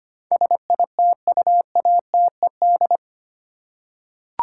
Morse code, heard as a keyed tone, about 25 wpm, 700 Hz. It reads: SITUATED
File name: Eight Letter Words - 25wpm